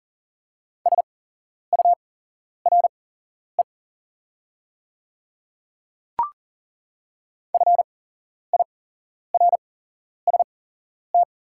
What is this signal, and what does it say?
Morse code, heard as a keyed tone, about 40 wpm, spaced as for 11 wpm, 690 Hz. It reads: SURE FIRST
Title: 8x Character Spacing - Top 500 Words - 40wpm